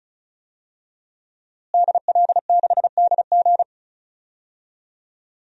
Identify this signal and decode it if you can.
Morse code, heard as a keyed tone, about 36 wpm, 690 Hz. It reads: DL6DG
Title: DXCC-100 Call Signs - 5 Characters - Repeated 3x - 35wpm